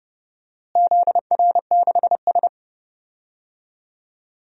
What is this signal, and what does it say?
Morse code, heard as a keyed tone, about 32 wpm, 700 Hz. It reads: ZR6S
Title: DXCC-100 Call Signs - 4 Characters - Repeated 3x - 30wpm